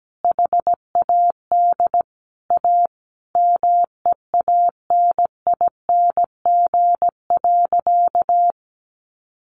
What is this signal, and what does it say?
Morse code, heard as a keyed tone, about 17 wpm, 700 Hz. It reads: HAD A MEANING.
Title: Sentences from Top 1400 Words 17wpm